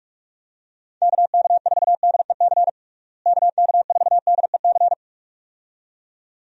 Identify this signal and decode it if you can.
Morse code, heard as a keyed tone, about 46 wpm, 700 Hz. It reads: KK4DEC KK4DEC
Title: USA Call Signs - 6 Characters - Repeated 2x - 45wpm